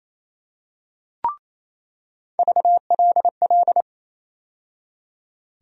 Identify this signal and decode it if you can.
Morse code, heard as a keyed tone, about 28 wpm, 700 Hz. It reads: VLL
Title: USA Call Sign Postfixes - 3 Letters 28wpm